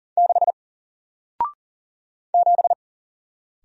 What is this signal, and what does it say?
Morse code, heard as a keyed tone, about 40 wpm, 700 Hz. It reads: 6 7
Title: Single Letter-Number 40wpm